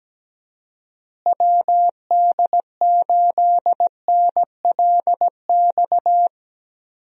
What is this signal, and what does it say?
Morse code, heard as a keyed tone, about 17 wpm, 700 Hz. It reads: WD8NLX
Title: USA Call Signs - 6 Characters - Repeated 2x - 17wpm